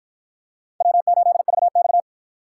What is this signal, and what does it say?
Morse code, heard as a keyed tone, about 52 wpm, 700 Hz. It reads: W8VX